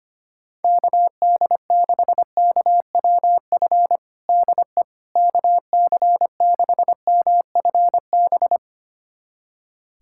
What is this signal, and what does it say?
Morse code, heard as a keyed tone, about 25 wpm, 700 Hz. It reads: KD6KWF DE KC6MFB
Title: USA Callsign DE Callsign - 6 Characters - 25wpm